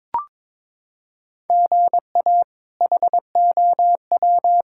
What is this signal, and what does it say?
Morse code, heard as a keyed tone, about 21 wpm, 700 Hz. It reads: GA HOW
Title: K1USN SST QSO Practice - 22wpm